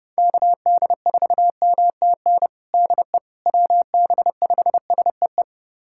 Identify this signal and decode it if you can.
Morse code, heard as a keyed tone, about 32 wpm, 700 Hz. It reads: KD4MTN DE WB5SEE
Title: USA Callsign DE Callsign - 6 Characters - 30wpm